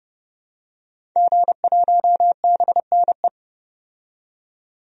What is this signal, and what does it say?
Morse code, heard as a keyed tone, about 32 wpm, 700 Hz. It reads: G1BNE
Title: DXCC-100 Call Signs - 5 Characters - 30wpm